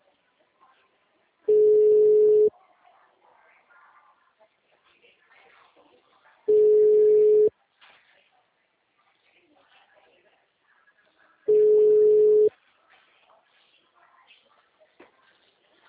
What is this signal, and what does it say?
Ringback tone (Europe, 425 Hz), with speech.
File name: out- -2006- - - .29